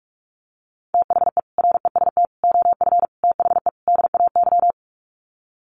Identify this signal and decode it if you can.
Morse code, heard as a keyed tone, about 44 wpm, 700 Hz. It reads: THE REST OF THE DAY
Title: Sets of 5 Words 45wpm